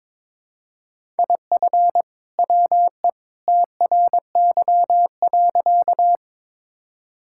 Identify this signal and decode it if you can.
Morse code, heard as a keyed tone, about 21 wpm, 700 Hz. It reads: IF WE TRY.
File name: practice / 22wpm / Sentences with 2-Word Phrases - 22wpm